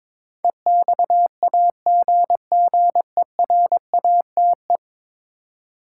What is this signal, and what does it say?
Morse code, heard as a keyed tone, about 22 wpm, 700 Hz. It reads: EXAGGERATE
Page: morse-code-ninja-practice-22wpm